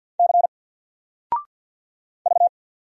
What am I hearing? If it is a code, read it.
Morse code, heard as a keyed tone, about 48 wpm, 700 Hz. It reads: C V
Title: Single Character (Letter:Number:Punctuation:Prosign) Alt 50wpm